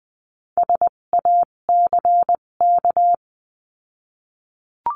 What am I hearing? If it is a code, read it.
Morse code, heard as a keyed tone, about 20 wpm, spaced as for 17 wpm, 700 Hz. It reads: SACK
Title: 257 Lesson 42b - Words 14-20wpm